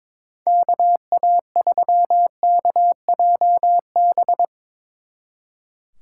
Morse code, heard as a keyed tone, about 24 wpm, 700 Hz. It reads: KA3KJB
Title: Speed Racing - USA Call Signs 22wpm